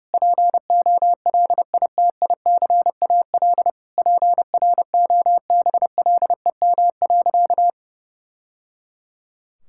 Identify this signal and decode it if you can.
Morse code, heard as a keyed tone, about 30 wpm, 700 Hz. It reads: POLITICAL PROBLEM.
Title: Sentences from Top 1000 Words - Review Part 5 - 30wpm